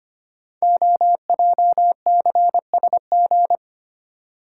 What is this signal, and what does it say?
Morse code, heard as a keyed tone, about 25 wpm, 700 Hz. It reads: OJCSG
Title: Code Groups - 5 Letters - 25wpm